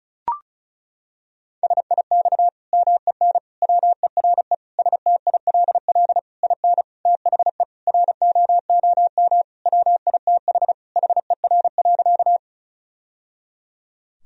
Morse code, heard as a keyed tone, about 36 wpm, 700 Hz. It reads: SIX MEN WERE STILL IN THE ROOM WITH HER.
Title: Sentences from Top 1000 Words - Review Part 2 - 35wpm